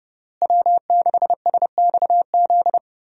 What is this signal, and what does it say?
Morse code, heard as a keyed tone, about 30 wpm, 700 Hz. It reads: W6SXZ
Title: USA Call Signs - 5 Characters - Repeated 3x - 30wpm